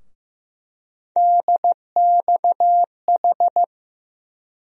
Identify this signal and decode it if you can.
Morse code, heard as a keyed tone, about 15 wpm, 700 Hz. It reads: DXH
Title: USA Call Sign Postfixes - 3 Letters 15wpm